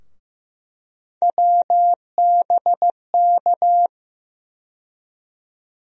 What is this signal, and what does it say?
Morse code, heard as a keyed tone, about 15 wpm, 700 Hz. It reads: WBK
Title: USA Call Sign Postfixes - 3 Letters 15wpm